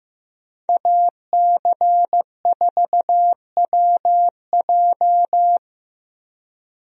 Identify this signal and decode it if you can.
Morse code, heard as a keyed tone, about 15 wpm, 700 Hz. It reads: AC4WJ
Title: USA Call Signs - 5 Characters - Repeated 2x - 15wpm